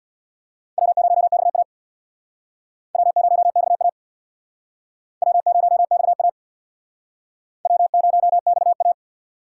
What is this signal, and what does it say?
Morse code, heard as a keyed tone, about 61 wpm, 700 Hz. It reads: W0XA W0XA W0XA W0XA
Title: Speed Racing - USA 1x2 Call Signs 50wpm